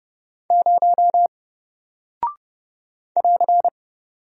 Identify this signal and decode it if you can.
Morse code, heard as a keyed tone, about 32 wpm, 700 Hz. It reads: 0 <AR>
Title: Single Character (Letter:Number:Punctuation:Prosign) Alt 30wpm